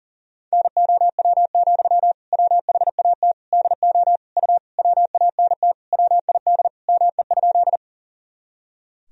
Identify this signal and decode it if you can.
Morse code, heard as a keyed tone, about 40 wpm, 700 Hz. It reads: NOW, WHAT DO U WANT WID ME?
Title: Sentences from Top 200 Words and Abbreviations 40wpm